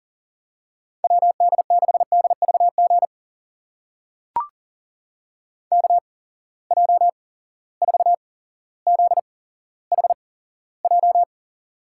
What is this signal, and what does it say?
Morse code, heard as a keyed tone, about 39 wpm, spaced as for 17 wpm, 700 Hz. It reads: WD6DVG KJ4ZHJ
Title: USA Call Signs - 6 Characters - 8x Character Spacing - 40wpm